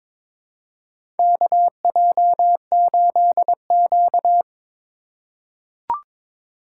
Morse code, heard as a keyed tone, about 23 wpm, 700 Hz. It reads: KJ8Q
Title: USA Call Signs - 4 Characters - 3x Character Spacing - 22wpm